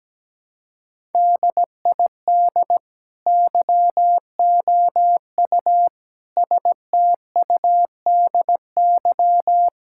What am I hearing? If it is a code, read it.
Morse code, heard as a keyed tone, about 17 wpm, 700 Hz. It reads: DID YOU STUDY
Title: Sentences from Top 1000 Words - Review Part 6 - 17wpm